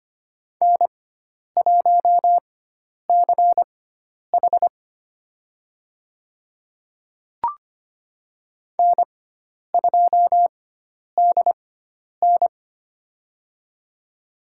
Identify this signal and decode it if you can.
Morse code, heard as a keyed tone, about 25 wpm, spaced as for 10 wpm, 700 Hz. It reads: N1CH N2DN
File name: Lesson 16g - Letter c - Call Signs 25wpm